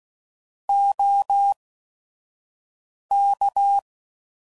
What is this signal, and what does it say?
Morse code, heard as a keyed tone, about 16 wpm, spaced as for 5 wpm, 800 Hz. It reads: OK
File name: CW Code Groups @5WPM / QSOs / WD0FBT 2